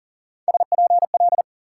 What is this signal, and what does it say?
Morse code, heard as a keyed tone, about 53 wpm, 700 Hz. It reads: SPL